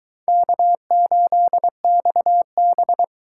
Morse code, heard as a keyed tone, about 22 wpm, 700 Hz. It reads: K8XB